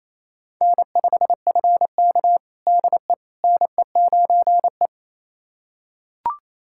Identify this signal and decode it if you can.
Morse code, heard as a keyed tone, about 27 wpm, 700 Hz. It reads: N5FK DE NE9E
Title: USA Callsign DE Callsign - 4 Characters - 28wpm